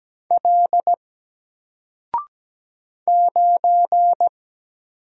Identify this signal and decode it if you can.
Morse code, heard as a keyed tone, about 17 wpm, 700 Hz. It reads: L 9
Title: Single Letter-Number Alt 17wpm